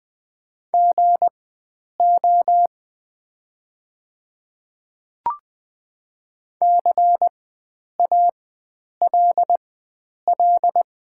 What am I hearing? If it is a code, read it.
Morse code, heard as a keyed tone, about 20 wpm, spaced as for 9 wpm, 700 Hz. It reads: GO CALL
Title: 4x Character Spacing - Top 100 Words - 20wpm